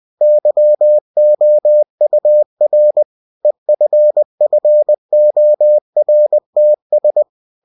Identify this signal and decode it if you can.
Morse code, heard as a keyed tone, about 20 wpm, 600 Hz. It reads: YOUR EFFORTS
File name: cw-books / winnie-the-pooh / winnie-20wpm-page-0120